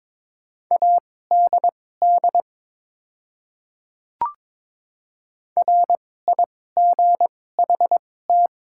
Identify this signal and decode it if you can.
Morse code, heard as a keyed tone, about 22 wpm, spaced as for 16 wpm, 700 Hz. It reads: ADD RIGHT
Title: 2x Character Spacing - Top 200 Words - 22wpm